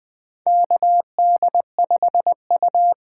Morse code, heard as a keyed tone, about 20 wpm, 700 Hz. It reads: KD5U
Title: USA Call Signs - 6 Characters - 8x Character Spacing - 20wpm